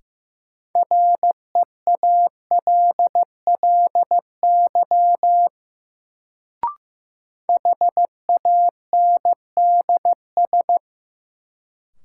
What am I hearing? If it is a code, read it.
Morse code, heard as a keyed tone, about 15 wpm, 700 Hz. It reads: REALLY HANDS
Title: Sets of 4 Words - New Word Drill - 15wpm